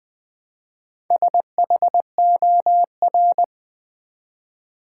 Morse code, heard as a keyed tone, about 20 wpm, 700 Hz. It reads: SHOR
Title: Build A Word - 1 Letter At A Time - 5-Letter Words - 20wpm